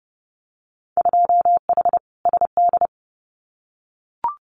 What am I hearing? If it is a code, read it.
Morse code, heard as a keyed tone, about 30 wpm, 700 Hz. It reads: 2H SD
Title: Winter Field Day Exchanges 30wpm